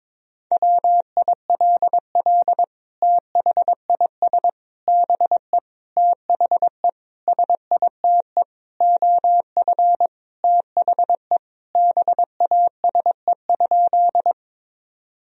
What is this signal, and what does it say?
Morse code, heard as a keyed tone, about 22 wpm, 700 Hz. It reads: WILL THIS BE THE SITE OF THE BASE?
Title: Sentences from Top 1000 Words - Review Part 1 - 22wpm